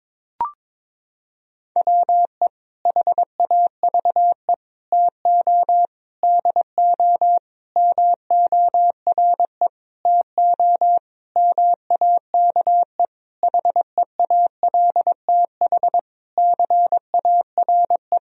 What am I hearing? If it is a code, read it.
Morse code, heard as a keyed tone, about 22 wpm, 700 Hz. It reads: WE HAVE TO DO MORE TO MAKE HEALTH CARE
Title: Sentences from Top 1000 Words - Review Part 3 - 22wpm